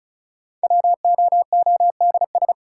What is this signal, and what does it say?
Morse code, heard as a keyed tone, about 36 wpm, 700 Hz. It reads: WOODS
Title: Sentences from Top 1000 Words - Review Part 2 - New Word Drill - 35wpm